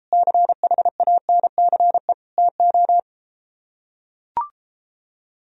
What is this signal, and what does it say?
Morse code, heard as a keyed tone, about 32 wpm, 700 Hz. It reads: CHANCE TO